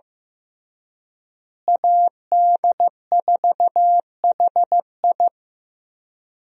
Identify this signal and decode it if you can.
Morse code, heard as a keyed tone, about 15 wpm, 700 Hz. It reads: AD4HI
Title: USA Call Signs - 5 Characters - Repeated 3x - 15wpm